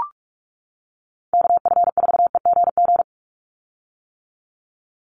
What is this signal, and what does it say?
Morse code, heard as a keyed tone, about 44 wpm, 700 Hz. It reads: KF4EGG
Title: USA Call Signs 45wpm